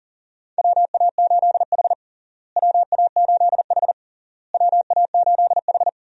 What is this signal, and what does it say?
Morse code, heard as a keyed tone, about 40 wpm, 700 Hz. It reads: WA8H WA8H WA8H
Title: USA Call Signs - 4 Characters - Repeated 3x - 40wpm